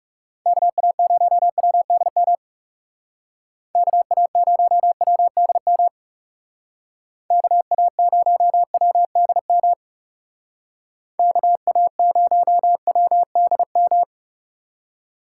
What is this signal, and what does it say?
Morse code, heard as a keyed tone, about 39 wpm, 700 Hz. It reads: KA0WDM KA0WDM KA0WDM KA0WDM